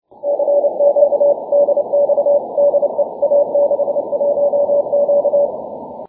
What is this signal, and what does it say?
Morse code, heard as a keyed tone, about 29 wpm, 570 Hz. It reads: CQ DX DE A61Q
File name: wp / a61q 6m cw 2 jun 2011